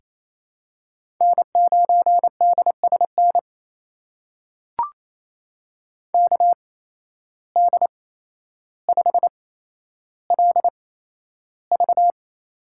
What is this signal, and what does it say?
Morse code, heard as a keyed tone, about 27 wpm, spaced as for 12 wpm, 700 Hz. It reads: N9DSN KD5LV
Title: USA Call Signs - 5 Characters - 8x Character Spacing - 28wpm